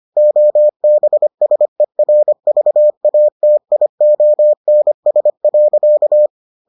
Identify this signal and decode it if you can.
Morse code, heard as a keyed tone, about 25 wpm, 600 Hz. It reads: OBSERVATIONS.